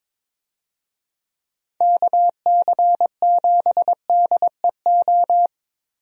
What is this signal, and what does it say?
Morse code, heard as a keyed tone, about 22 wpm, 700 Hz. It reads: KC7DEO
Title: USA Call Signs - 6 Characters - Repeated 2x - 22wpm